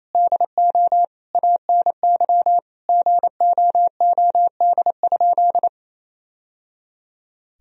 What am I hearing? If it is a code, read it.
Morse code, heard as a keyed tone, about 28 wpm, 700 Hz. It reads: DO ANY GOOD?
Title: Sentences from Top 1000 Words - Review Part 4 - 28wpm